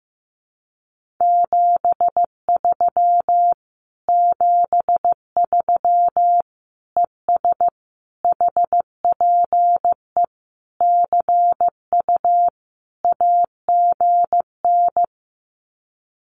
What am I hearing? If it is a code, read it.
Morse code, heard as a keyed tone, about 15 wpm, 700 Hz. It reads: 73 73 ES HPE CU AGN
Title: Common QSO Phrases Encore 15wpm